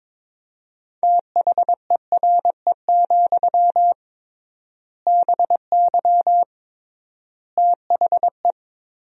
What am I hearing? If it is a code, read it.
Morse code, heard as a keyed tone, about 22 wpm, 700 Hz. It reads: THERE, BY THE
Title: 3x Word-Spacing - Sentences from Top 500 Words - 22wpm